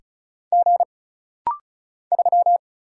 Morse code, heard as a keyed tone, about 36 wpm, 700 Hz. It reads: G 3
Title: Single Letter-Number - Rapid-Fire 35wpm